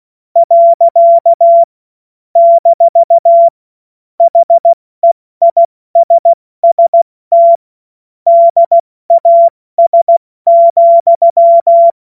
Morse code, heard as a keyed tone, about 16 wpm, 680 Hz. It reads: . - HEISST DAS,
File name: Morsekurs / Witze / Witz 26 März 2024 70 F80